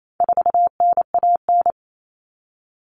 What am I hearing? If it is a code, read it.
Morse code, heard as a keyed tone, about 28 wpm, 700 Hz. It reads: 4NAN
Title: Common Cut Numbers - 4-Digit 28wpm